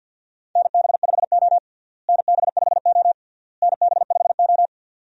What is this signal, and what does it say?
Morse code, heard as a keyed tone, about 53 wpm, 700 Hz. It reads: NB5O NB5O NB5O
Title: USA Call Signs - 4 Characters - Repeated 3x - 50wpm